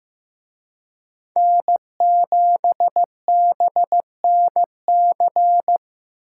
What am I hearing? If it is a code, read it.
Morse code, heard as a keyed tone, about 18 wpm, 700 Hz. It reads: N7BNC
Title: Speed Racing - USA Call Signs 15wpm